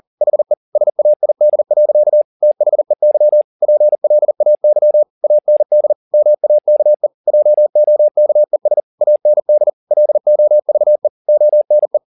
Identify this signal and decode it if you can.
Morse code, heard as a keyed tone, about 40 wpm, 600 Hz. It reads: HE SAID. THEY PLAY AND MAKE JOKES AND LOVE ONE